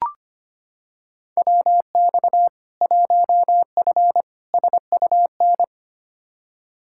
Morse code, heard as a keyed tone, about 25 wpm, 700 Hz. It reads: WX 1F SUN
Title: Practice QSO - Exchange 2 - Info - 25wpm